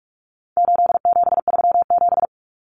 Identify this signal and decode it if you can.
Morse code, heard as a keyed tone, about 44 wpm, 700 Hz. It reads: 8737
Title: Common Cut Numbers - 4-Digit 45wpm